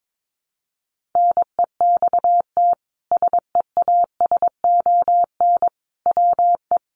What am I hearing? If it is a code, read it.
Morse code, heard as a keyed tone, about 22 wpm, 700 Hz. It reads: NEXT SEASON WE
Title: Sentences from Top 1400 Words 22wpm